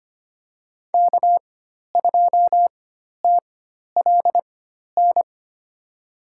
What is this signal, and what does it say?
Morse code, heard as a keyed tone, about 25 wpm, spaced as for 12 wpm, 700 Hz. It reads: K2TLN
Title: USA Call Signs - 4x Character Spacing - 25wpm